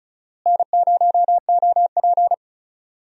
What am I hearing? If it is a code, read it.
Morse code, heard as a keyed tone, about 36 wpm, 700 Hz. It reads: N0OP
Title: USA Call Signs - 4 Characters - Repeated 2x - 35wpm